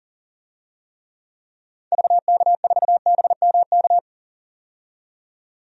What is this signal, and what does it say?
Morse code, heard as a keyed tone, about 37 wpm, 700 Hz. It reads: VK4BMK
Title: DXCC-100 Call Signs - 6 Characters - 40wpm